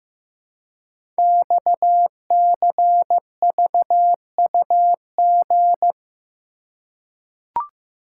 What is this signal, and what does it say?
Morse code, heard as a keyed tone, about 15 wpm, 700 Hz. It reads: XCVUG
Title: Code Groups - 5 Letter:Numbers - 15wpm